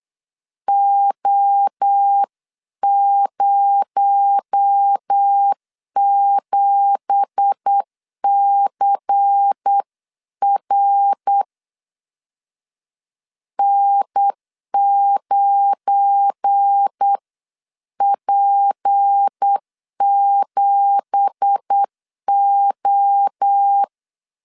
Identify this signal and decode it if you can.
Morse code, heard as a keyed tone, about 8 wpm, 800 Hz. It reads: O07CR N9P7O
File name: CW / DET / 1 9 1500C S C 8wpm 5653